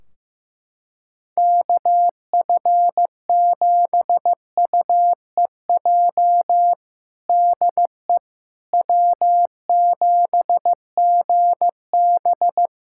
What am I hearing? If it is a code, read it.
Morse code, heard as a keyed tone, about 15 wpm, 700 Hz. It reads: KF7UEJ DE W7GB
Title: Practice QSO - Exchange 4 - Ending - 15wpm